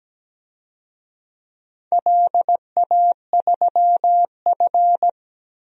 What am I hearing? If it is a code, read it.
Morse code, heard as a keyed tone, about 17 wpm, 700 Hz. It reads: LA3F
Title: DXCC-100 Call Signs - 4 Characters - 17wpm